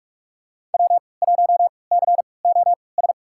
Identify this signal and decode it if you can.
Morse code, heard as a keyed tone, about 46 wpm, spaced as for 25 wpm, 700 Hz. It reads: W1COS
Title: USA Call Signs - 5 Characters - 3x Character Spacing - 45wpm